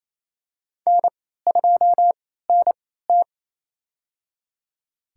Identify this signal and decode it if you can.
Morse code, heard as a keyed tone, about 28 wpm, spaced as for 16 wpm, 700 Hz. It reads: N2NT
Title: USA Call Signs - 4 Characters - 3x Character Spacing - 28wpm